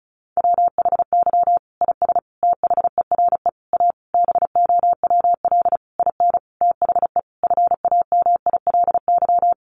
Morse code, heard as a keyed tone, about 34 wpm, 700 Hz. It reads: WHY IS THERE A BOWL IN THE FAMILY
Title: Sentences from Top 1200 Words 35wpm